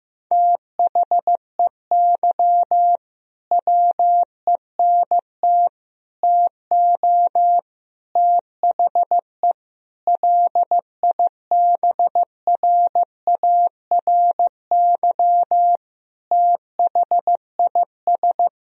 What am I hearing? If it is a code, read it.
Morse code, heard as a keyed tone, about 15 wpm, 700 Hz. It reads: THEY WENT TO THE LIBRARY THIS
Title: Sentences from Top 1000 Words - Review Part 4 - 15wpm